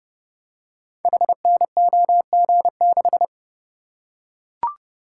Morse code, heard as a keyed tone, about 29 wpm, 700 Hz. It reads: HNOG6
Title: Code Groups - 5 Letter:Numbers - 30wpm